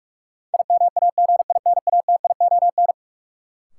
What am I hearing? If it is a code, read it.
Morse code, heard as a keyed tone, about 41 wpm, 700 Hz. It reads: IMAGINATION
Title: Sentences from Top 1300 Words - New Word Drill Alt - 45wpm